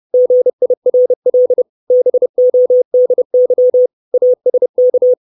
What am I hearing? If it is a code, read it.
Morse code, heard as a keyed tone, about 30 wpm, 500 Hz. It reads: GIRL BODY ASK
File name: 632 Words 30WPM